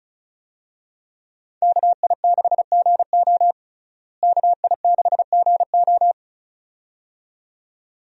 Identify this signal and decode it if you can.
Morse code, heard as a keyed tone, about 36 wpm, 700 Hz. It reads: KI6GO KI6GO
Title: USA Call Signs - 5 Characters - Repeated 2x - 35wpm